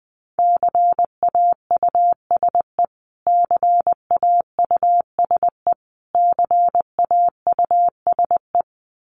Morse code, heard as a keyed tone, about 19 wpm, 700 Hz. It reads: CAUSE CAUSE CAUSE
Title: Top 200 Words Repeated 20wpm